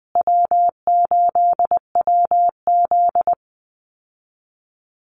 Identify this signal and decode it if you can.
Morse code, heard as a keyed tone, about 20 wpm, 700 Hz. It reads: W8WZ
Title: Easy Call Signs Repeated 3x - 20wpm